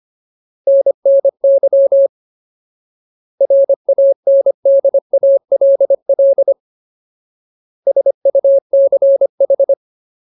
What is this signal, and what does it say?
Morse code, heard as a keyed tone, about 25 wpm, 550 Hz. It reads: NNY RANDALL SUCH
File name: Ultimate Word List / 25WPM Ultimate